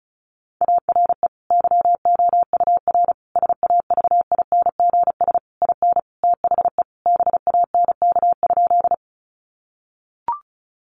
Morse code, heard as a keyed tone, about 34 wpm, 700 Hz. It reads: ARE YOUR SAVINGS IN THE BANK?
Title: Sentences from Top 1200 Words 35wpm